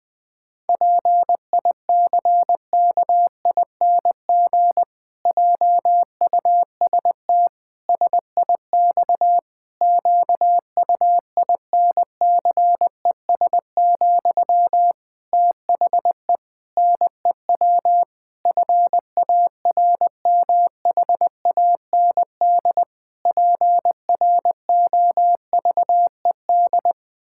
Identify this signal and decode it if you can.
Morse code, heard as a keyed tone, about 21 wpm, 700 Hz. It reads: PICKING JUST SIX QUINCES, THE NEW FARMHAND PROVED
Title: Pangram Challenge v2 - 20wpm